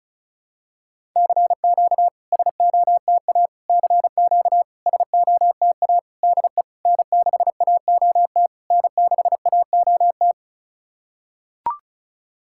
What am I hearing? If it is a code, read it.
Morse code, heard as a keyed tone, about 36 wpm, 700 Hz. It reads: CQ SOTA CQ SOTA DE N6AOT N6AOT
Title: SOTA QSO Practice 35wpm